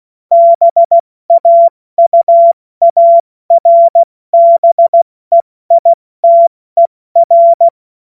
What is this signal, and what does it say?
Morse code, heard as a keyed tone, about 16 wpm, 680 Hz. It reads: BAUARBEITER
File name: Morsekurs / Witze / Witz 20 Februar 2024 70 F80